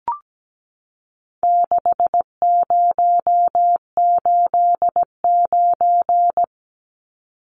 Morse code, heard as a keyed tone, about 17 wpm, 700 Hz. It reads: 6089
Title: Numbers - 4-Digit 17wpm